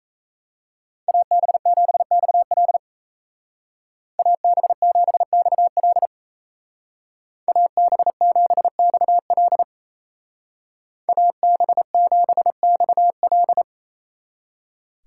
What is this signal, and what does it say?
Morse code, heard as a keyed tone, about 34 wpm, 700 Hz. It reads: AB7XL AB7XL AB7XL AB7XL
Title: Speed Racing - USA 2x2 Call Signs 28wpm